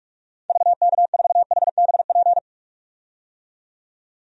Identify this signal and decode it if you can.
Morse code, heard as a keyed tone, about 46 wpm, 700 Hz. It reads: VK4HBP
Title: DXCC-100 Call Signs - 6 Characters - 45wpm